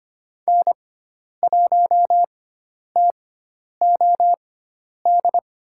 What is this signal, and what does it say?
Morse code, heard as a keyed tone, about 25 wpm, spaced as for 10 wpm, 700 Hz. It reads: N1TOD
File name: Lesson 16g - Letter c - Call Signs 25wpm